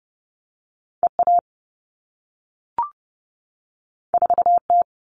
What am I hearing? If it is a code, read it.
Morse code, heard as a keyed tone, about 30 wpm, 700 Hz. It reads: EA 4T
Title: All Cut Numbers - 2-Digit 30wpm